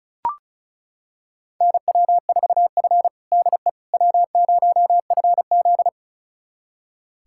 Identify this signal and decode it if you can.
Morse code, heard as a keyed tone, about 34 wpm, 700 Hz. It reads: NW4F DE W0FZ
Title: USA Callsign DE Callsign - 4 Characters - 35wpm